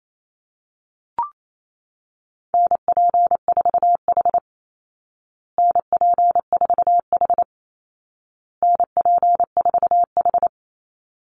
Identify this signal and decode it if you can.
Morse code, heard as a keyed tone, about 28 wpm, 700 Hz. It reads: NP4H NP4H NP4H
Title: Call Signs Repeated 2x - 28wpm